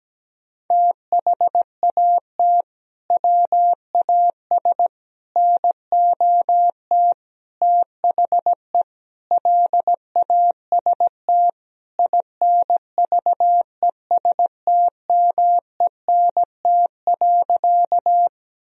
Morse code, heard as a keyed tone, about 17 wpm, 700 Hz. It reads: THAT WAS NOT THE LAST INVESTMENT.
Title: Sentences from Top 1000 Words - Review Part 4 - 17wpm